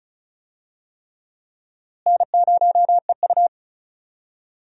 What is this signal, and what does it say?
Morse code, heard as a keyed tone, about 36 wpm, 700 Hz. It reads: N0EU
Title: USA Call Signs - 4 Characters - Repeated 3x - 35wpm